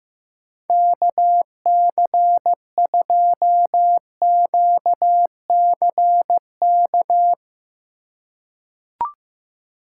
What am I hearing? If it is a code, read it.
Morse code, heard as a keyed tone, about 15 wpm, 700 Hz. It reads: KC2QCK
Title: USA Call Signs - 6 Characters - 2x Character Spacing - 15wpm